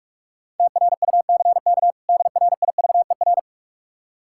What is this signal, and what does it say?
Morse code, heard as a keyed tone, about 48 wpm, 700 Hz. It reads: TRUCK DRIVER
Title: K1USN SST QSO Practice 45wpm